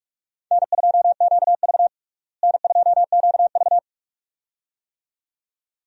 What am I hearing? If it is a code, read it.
Morse code, heard as a keyed tone, about 41 wpm, 700 Hz. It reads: N2QV N2QV
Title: USA Call Signs - 4 Characters - Repeated 2x - 45wpm